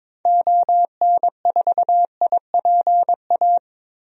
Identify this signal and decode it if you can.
Morse code, heard as a keyed tone, about 22 wpm, 700 Hz. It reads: ON4IPA
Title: DXCC-100 Call Signs - 6 Characters - Repeated 3x - 22wpm